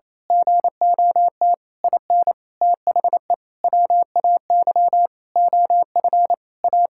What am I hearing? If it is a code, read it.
Morse code, heard as a keyed tone, about 28 wpm, 700 Hz. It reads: GOT IN THE WAY OF A
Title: Sentences from Top 1000 Words - Review Part 6 - 28wpm